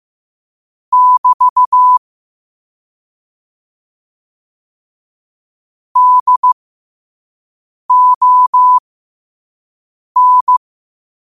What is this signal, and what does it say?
Morse code, heard as a keyed tone, about 15 wpm, spaced as for 5 wpm, 1000 Hz. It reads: <BT> DON
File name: qotdfiles / 5wpm